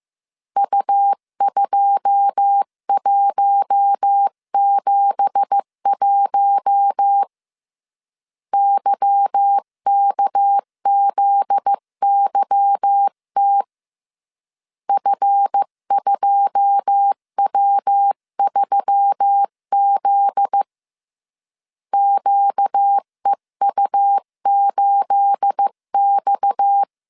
Morse code, heard as a keyed tone, about 15 wpm, 800 Hz. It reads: U2171 YKZYT F2W3Z QEU8X
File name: CW / DET / 1 9 1500C S B 14wpm 3302